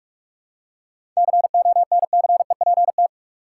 Morse code, heard as a keyed tone, about 42 wpm, 700 Hz. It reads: CONCEPT
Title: Seven Letter Words - 45wpm